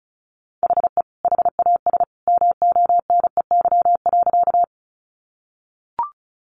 Morse code, heard as a keyed tone, about 34 wpm, 700 Hz. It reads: HE HAS MONEY.
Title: Sentences from Top 400 Words 35wpm